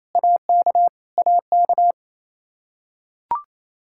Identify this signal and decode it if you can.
Morse code, heard as a keyed tone, about 28 wpm, 700 Hz. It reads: AK AK
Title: Practice QSO - Exchange 1 - Beginning - 28wpm